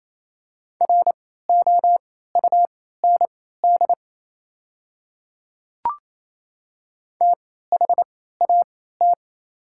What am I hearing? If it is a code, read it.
Morse code, heard as a keyed tone, about 28 wpm, spaced as for 16 wpm, 700 Hz. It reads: ROUND THAT
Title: 3x Character Spacing - Top 200 Words - 28wpm